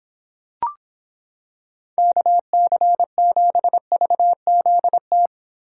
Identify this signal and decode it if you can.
Morse code, heard as a keyed tone, about 26 wpm, 700 Hz. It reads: KC7VZT